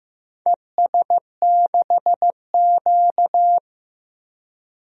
Morse code, heard as a keyed tone, about 15 wpm, 700 Hz. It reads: ES6Q
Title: DXCC-100 Call Signs - 4 Characters - 15wpm